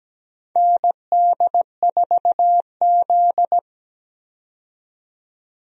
Morse code, heard as a keyed tone, about 17 wpm, 700 Hz. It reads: ND4Z
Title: USA Call Signs - 4 Characters - 17wpm